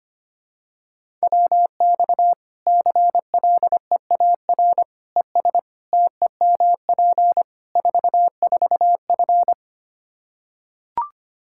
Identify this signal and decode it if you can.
Morse code, heard as a keyed tone, about 25 wpm, 700 Hz. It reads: WX CLEAR ES TEMP 44F
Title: Practice QSO - Exchange 2 - Info - 25wpm